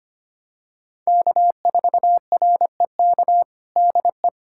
Morse code, heard as a keyed tone, about 25 wpm, 700 Hz. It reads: K4REK DE
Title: USA Callsign DE Callsign - 5 Characters - 25wpm